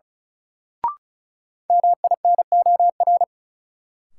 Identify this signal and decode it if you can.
Morse code, heard as a keyed tone, about 36 wpm, 700 Hz. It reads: MINOR
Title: Sentences from Top 1500 Words - New Word Drill Alt - 35wpm